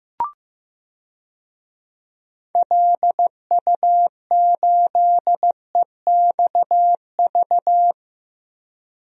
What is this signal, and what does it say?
Morse code, heard as a keyed tone, about 16 wpm, 700 Hz. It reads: LU8EXV
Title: DXCC-100 Call Signs - 6 Characters - 15wpm